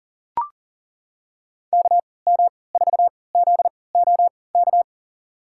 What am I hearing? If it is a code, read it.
Morse code, heard as a keyed tone, about 40 wpm, spaced as for 23 wpm, 700 Hz. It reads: KM4ZOK
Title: USA Call Signs - 6 Characters - 3x Character Spacing - 40wpm